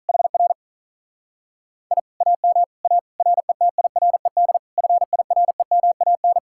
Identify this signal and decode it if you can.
Morse code, heard as a keyed tone, about 44 wpm, 700 Hz. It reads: HR I AM A RETIRED FIREMAN